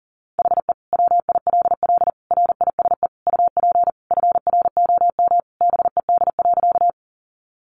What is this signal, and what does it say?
Morse code, heard as a keyed tone, about 40 wpm, 700 Hz. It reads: HE WILL RISE UP FROM BED.
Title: Sentences from Top 900 Words 40wpm